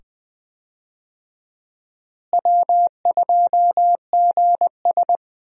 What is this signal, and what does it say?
Morse code, heard as a keyed tone, about 21 wpm, 700 Hz. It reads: W2GS
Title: USA Call Signs - 4 Characters - Repeated 2x - 20wpm